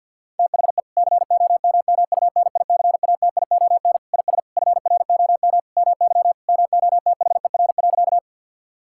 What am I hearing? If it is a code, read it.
Morse code, heard as a keyed tone, about 50 wpm, 700 Hz. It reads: THE COMMUNICATION IS FROM MY MOTHER.
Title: Sentences from Top 1000 Words - Review Part 6 - 50wpm